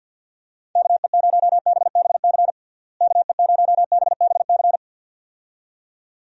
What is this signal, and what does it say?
Morse code, heard as a keyed tone, about 50 wpm, 700 Hz. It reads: KE0BBC KE0BBC
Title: USA Call Signs - 6 Characters - Repeated 2x - 50wpm